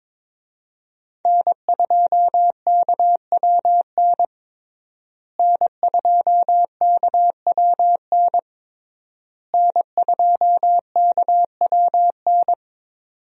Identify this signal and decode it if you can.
Morse code, heard as a keyed tone, about 22 wpm, 700 Hz. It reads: N2KWN N2KWN N2KWN
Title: USA Call Signs - 5 Characters - Repeated 3x - 22wpm